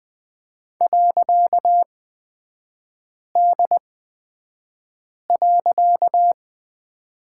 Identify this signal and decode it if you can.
Morse code, heard as a keyed tone, about 20 wpm, spaced as for 5 wpm, 700 Hz. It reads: .D.
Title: 255 Lesson 41e - Period - Three Characters 20wpm